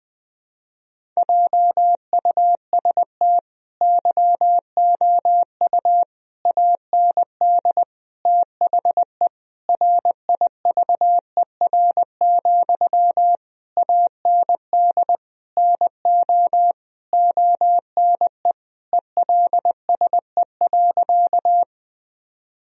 Morse code, heard as a keyed tone, about 20 wpm, 700 Hz. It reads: JUST YOU AND THE RIVER, AND NO ONE ELSE.
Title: Sentences from Top 1000 Words - Review Part 1 - 20wpm